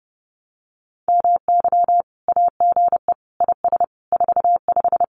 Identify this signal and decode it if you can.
Morse code, heard as a keyed tone, about 30 wpm, 700 Hz. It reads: MY AGE IS 45
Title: Common QSO Phrases Encore 30wpm